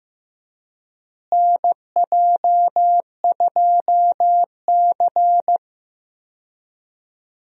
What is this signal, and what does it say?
Morse code, heard as a keyed tone, about 15 wpm, 700 Hz. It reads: NJ2C
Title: USA Call Signs - 4 Characters - Repeated 3x - 15wpm